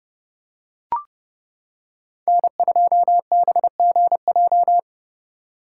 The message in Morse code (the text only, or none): N2BGJ